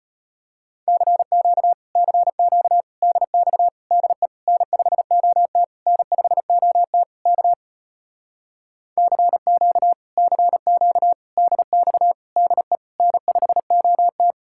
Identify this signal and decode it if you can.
Morse code, heard as a keyed tone, about 33 wpm, 700 Hz. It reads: CQ CQ DX DE N5OT N5OT K CQ CQ DX DE N5OT